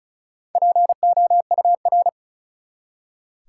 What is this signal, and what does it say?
Morse code, heard as a keyed tone, about 36 wpm, 700 Hz. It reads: POUR
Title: Four Letter Words - 35wpm